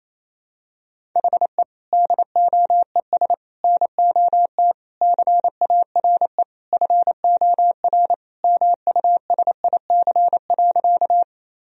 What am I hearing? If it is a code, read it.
Morse code, heard as a keyed tone, about 28 wpm, 700 Hz. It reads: HE DOES NOT CARE FOR MUSIC.
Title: Sentences from Top 1000 Words - Review Part 1 - 28wpm